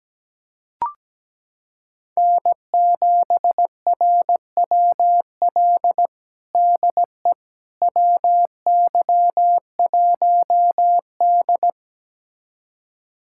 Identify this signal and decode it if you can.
Morse code, heard as a keyed tone, about 17 wpm, 700 Hz. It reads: N7RWL DE WY1D
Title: USA Callsign DE Callsign 17wpm